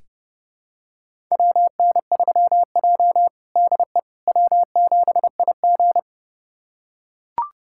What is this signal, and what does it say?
Morse code, heard as a keyed tone, about 30 wpm, 700 Hz. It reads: WN3J DE W7IG
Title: USA Callsign DE Callsign - 4 Characters - 30wpm